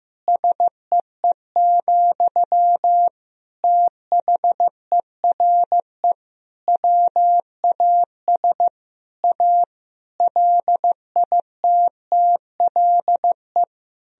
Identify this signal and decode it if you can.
Morse code, heard as a keyed tone, about 15 wpm, 700 Hz. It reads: SEE, THERE WAS A LITTLE
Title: Sentences from Top 1000 Words - Review Part 1 - 15wpm